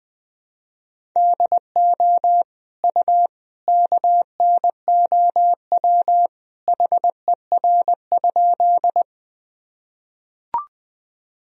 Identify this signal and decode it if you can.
Morse code, heard as a keyed tone, about 20 wpm, 700 Hz. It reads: DO U KNOW HER?
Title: Sentences from Top 100 Words and Abbreviations 20wpm